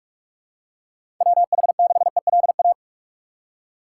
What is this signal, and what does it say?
Morse code, heard as a keyed tone, about 46 wpm, 700 Hz. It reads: WH6ELA
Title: USA Call Signs - 6 Characters - 4x Character Spacing - 45wpm